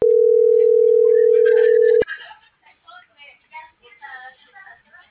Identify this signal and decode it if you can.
Ringback tone (North America), 440 + 480 Hz.